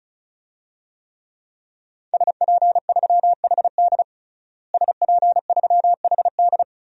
Morse code, heard as a keyed tone, about 36 wpm, 700 Hz. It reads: SP3HD SP3HD
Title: DXCC-100 Call Signs - 5 Characters - Repeated 3x - 35wpm